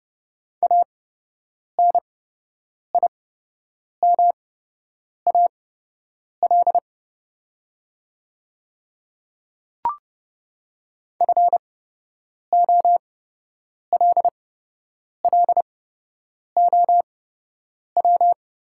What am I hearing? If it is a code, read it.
Morse code, heard as a keyed tone, about 30 wpm, spaced as for 8 wpm, 700 Hz. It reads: ANIMAL FOLLOW
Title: 8x Character Spacing - Top 200 Words - 30wpm